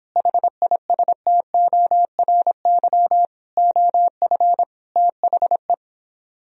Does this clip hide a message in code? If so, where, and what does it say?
Yes: Morse code from 0.2 to 5.7 s, heard as a keyed tone, about 29 wpm, 700 Hz. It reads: HISTORY OF THE